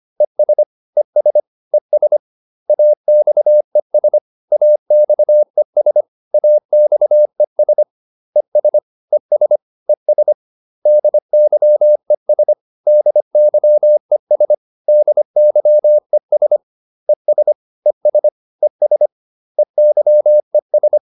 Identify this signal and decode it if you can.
Morse code, heard as a keyed tone, about 25 wpm, 600 Hz. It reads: ES ES ES AXES AXES AXES ES ES ES DYES DYES DYES ES ES ES EYES